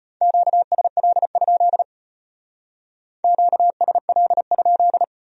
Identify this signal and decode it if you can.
Morse code, heard as a keyed tone, about 33 wpm, 700 Hz. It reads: QSL? QSL?